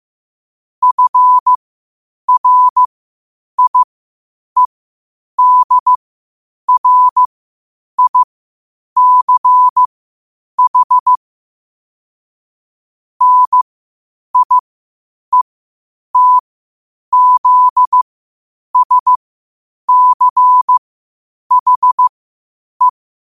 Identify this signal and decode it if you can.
Morse code, heard as a keyed tone, about 15 wpm, spaced as for 8 wpm, 1000 Hz. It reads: FRIEDRICH NIETZSCHE
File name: qotdfiles / 8wpm